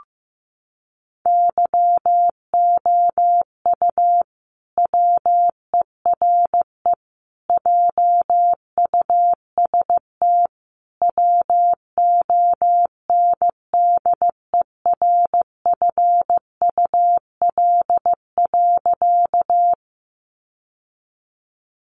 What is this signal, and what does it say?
Morse code, heard as a keyed tone, about 15 wpm, 700 Hz. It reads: YOU WERE JUST WONDERFUL.
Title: Sentences from Top 1100 Words 15wpm